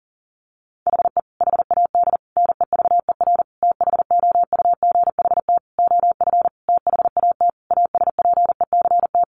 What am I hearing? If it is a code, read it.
Morse code, heard as a keyed tone, about 40 wpm, 700 Hz. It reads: HE HAD NEVER THOUGHT OF THAT ASPECT
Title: Sentences from Top 1300 Words 40wpm